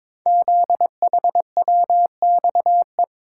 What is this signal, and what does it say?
Morse code, heard as a keyed tone, about 23 wpm, 700 Hz. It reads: ZHWXE
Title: Code Groups - 5 Letters - 22wpm